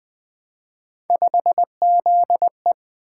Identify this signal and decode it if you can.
Morse code, heard as a keyed tone, about 20 wpm, 700 Hz. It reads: 5ZE
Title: Three Letter-Number Combinations - 20wpm